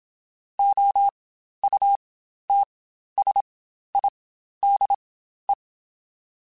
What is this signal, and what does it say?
Morse code, heard as a keyed tone, about 27 wpm, spaced as for 12 wpm, 800 Hz. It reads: OUTSIDE
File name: Spaced Characters / Spaced Characters-mp3s / Weather4 24 sp @25